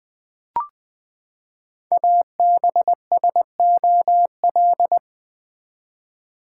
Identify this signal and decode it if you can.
Morse code, heard as a keyed tone, about 20 wpm, 700 Hz. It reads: ABSOL
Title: Build A Word - 5 Letters At A Time - 10-Letter Words - 20wpm